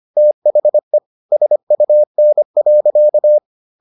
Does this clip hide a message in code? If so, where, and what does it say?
Yes: Morse code from 0.2 to 3.4 s, heard as a keyed tone, about 25 wpm, 600 Hz. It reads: THE SUN.